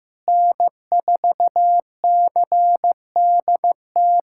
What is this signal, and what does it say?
Morse code, heard as a keyed tone, about 16 wpm, 700 Hz. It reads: N4CDT
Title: USA Callsign DE Callsign - 5 Characters - 15wpm